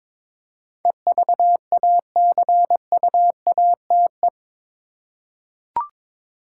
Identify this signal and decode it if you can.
Morse code, heard as a keyed tone, about 21 wpm, 700 Hz. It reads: EVACUATE
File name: practice / 22wpm / Eight Letter Words - 22wpm